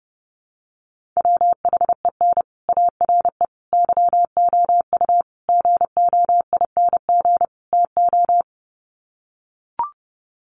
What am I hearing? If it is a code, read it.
Morse code, heard as a keyed tone, about 30 wpm, 700 Hz. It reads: WHEN ARE YOU GOING TO
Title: Sets of 5 Words Encore2 30wpm